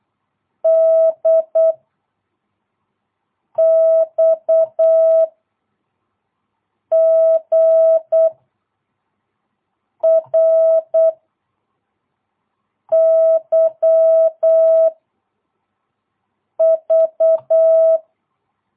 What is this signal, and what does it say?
Morse code, heard as a keyed tone, about 8 wpm, 650 Hz. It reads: D X G R Y V
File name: jh8cbh / 2022cw